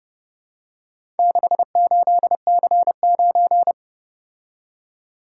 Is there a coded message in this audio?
Yes: Morse code from 1.2 to 3.7 s, heard as a keyed tone, about 29 wpm, 700 Hz. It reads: 68C9